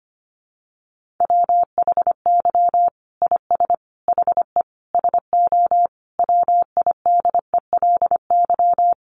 Morse code, heard as a keyed tone, about 25 wpm, 700 Hz. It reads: WHY IS HE SO WIDELY
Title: Sentences from Top 1200 Words 25wpm